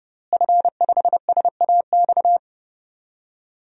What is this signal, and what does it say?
Morse code, heard as a keyed tone, about 30 wpm, 700 Hz. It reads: F5SAX
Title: DXCC-100 Call Signs - 5 Characters - Repeated 3x - 30wpm